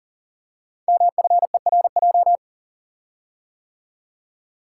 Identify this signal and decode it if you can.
Morse code, heard as a keyed tone, about 40 wpm, 690 Hz. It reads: MFERJ
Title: Code Groups - 5 Letters - 40wpm